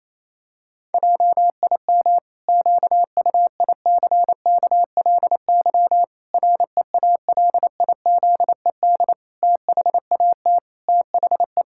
Morse code, heard as a keyed tone, about 28 wpm, 700 Hz. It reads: JIM QUICKLY REALIZED THAT THE
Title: Pangram Challenge v2 - 28wpm